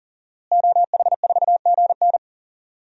Morse code, heard as a keyed tone, about 40 wpm, 700 Hz. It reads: OH4GN
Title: DXCC-100 Call Signs - 5 Characters - Repeated 3x - 40wpm